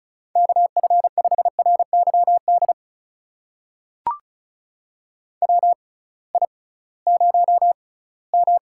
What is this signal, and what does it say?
Morse code, heard as a keyed tone, about 36 wpm, spaced as for 21 wpm, 700 Hz. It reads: KF5RYD WI0M
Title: USA Call Signs - 6x Character Spacing - 35wpm